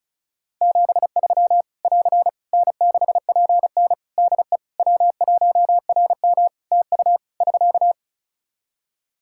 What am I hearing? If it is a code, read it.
Morse code, heard as a keyed tone, about 36 wpm, 700 Hz. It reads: 73 <AR> N6PN DE W1RM TU <SK>
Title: Practice QSO - Exchange 4 - Ending - 35wpm